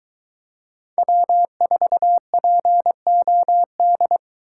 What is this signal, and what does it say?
Morse code, heard as a keyed tone, about 23 wpm, 700 Hz. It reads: W4POD